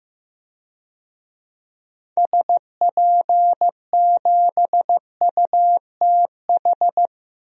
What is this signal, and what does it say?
Morse code, heard as a keyed tone, about 15 wpm, 700 Hz. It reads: SP7UTH
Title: DXCC-100 Call Signs - 6 Characters - Repeated 2x - 15wpm